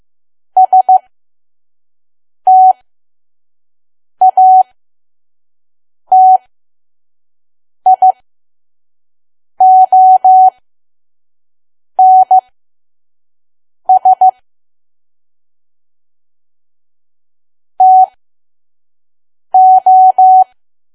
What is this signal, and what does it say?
Morse code, heard as a keyed tone, about 15 wpm, spaced as for 5 wpm, 750 Hz. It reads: STATIONS TO
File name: Archive / 5 WPM files / PM